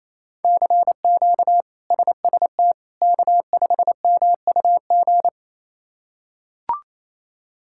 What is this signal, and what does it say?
Morse code, heard as a keyed tone, about 28 wpm, 700 Hz. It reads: CQ SST K5MUG
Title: K1USN SST QSO Practice - 28wpm